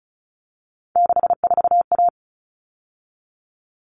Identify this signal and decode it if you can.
Morse code, heard as a keyed tone, about 34 wpm, 700 Hz. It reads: 64A
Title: All Cut Numbers - 3-Digit 35wpm